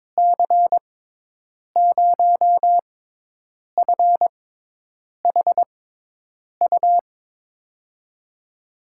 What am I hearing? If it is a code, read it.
Morse code, heard as a keyed tone, about 22 wpm, spaced as for 8 wpm, 700 Hz. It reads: C0FHU
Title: USA Call Signs - 6 Characters - 6x Character Spacing - 22wpm